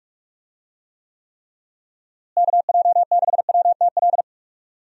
Morse code, heard as a keyed tone, about 41 wpm, 700 Hz. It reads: KJ6WTL
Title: USA Call Signs - 6 Characters - Repeated 3x - 45wpm